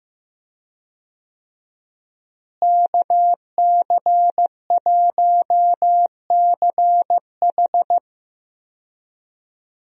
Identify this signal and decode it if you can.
Morse code, heard as a keyed tone, about 15 wpm, 700 Hz. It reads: KC1CH
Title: USA Call Signs - 5 Characters - Repeated 3x - 15wpm